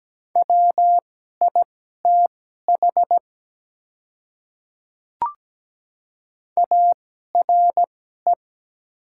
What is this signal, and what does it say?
Morse code, heard as a keyed tone, about 17 wpm, spaced as for 12 wpm, 700 Hz. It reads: WITH ARE
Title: 2x Character Spacing - Top 100 Words - 17wpm